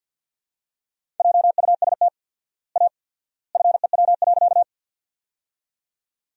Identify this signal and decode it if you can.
Morse code, heard as a keyed tone, about 51 wpm, 700 Hz. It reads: JUST A FEW.
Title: 4x Word-Spacing - Sentences from Top 300 Words - 50wpm